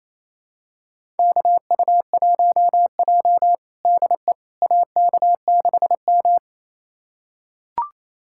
Morse code, heard as a keyed tone, about 28 wpm, 700 Hz. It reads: KU1J DE AK6M
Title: USA Callsign DE Callsign - 4 Characters - 28wpm